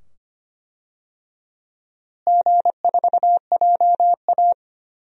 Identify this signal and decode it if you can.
Morse code, heard as a keyed tone, about 25 wpm, 700 Hz. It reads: G4JA
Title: DXCC-100 Call Signs - 4 Characters - Repeated 2x - 25wpm